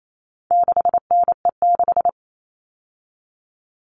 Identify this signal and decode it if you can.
Morse code, heard as a keyed tone, about 28 wpm, 700 Hz. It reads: 6NE6
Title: All Cut Numbers - 4-Digit 28wpm